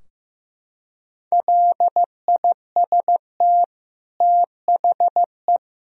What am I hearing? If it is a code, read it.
Morse code, heard as a keyed tone, about 15 wpm, 700 Hz. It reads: LIST THE
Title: Sentences from Top 1000 Words - Review Part 2 - 15wpm